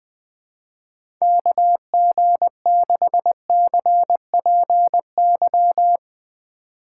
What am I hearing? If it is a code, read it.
Morse code, heard as a keyed tone, about 20 wpm, 700 Hz. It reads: KG6CPY
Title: USA Call Signs - 6 Characters - 3x Character Spacing - 20wpm